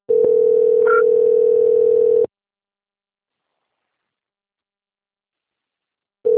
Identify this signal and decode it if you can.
Ringback tone (North America), 440 + 480 Hz.